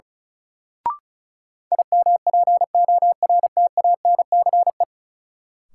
Morse code, heard as a keyed tone, about 36 wpm, 700 Hz. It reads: IMPORTANCE
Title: Sentences from Top 1200 Words - New Word Drill Alt - 35wpm